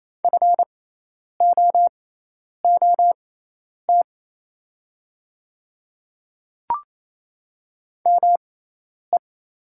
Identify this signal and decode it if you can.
Morse code, heard as a keyed tone, about 28 wpm, spaced as for 10 wpm, 700 Hz. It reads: FOOT ME
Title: 6x Character Spacing - Top 500 Words - 28wpm